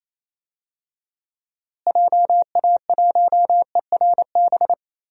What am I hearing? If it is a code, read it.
Morse code, heard as a keyed tone, about 28 wpm, 700 Hz. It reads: JA1ERB
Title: DXCC-100 Call Signs - 6 Characters - Repeated 3x - 28wpm